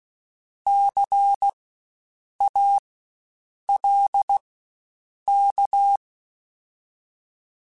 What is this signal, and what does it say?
Morse code, heard as a keyed tone, about 16 wpm, spaced as for 7 wpm, 800 Hz. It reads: CALK
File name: Spaced Alphas / Spaced Alphas mp3s / Cs sp @15